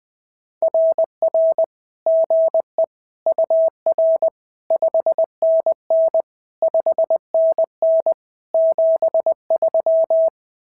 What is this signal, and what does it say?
Morse code, heard as a keyed tone, about 20 wpm, 650 Hz. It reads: RR GE UR 5NN 5NN 73
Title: SOTA QSO Practice 20wpm